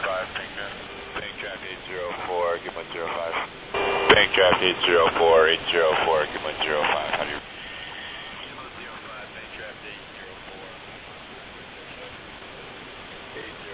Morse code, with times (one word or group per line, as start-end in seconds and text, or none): none